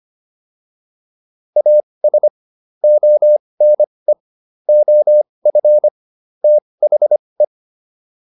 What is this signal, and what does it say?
Morse code, heard as a keyed tone, about 25 wpm, spaced as for 20 wpm, 600 Hz. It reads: AS ONE OF THE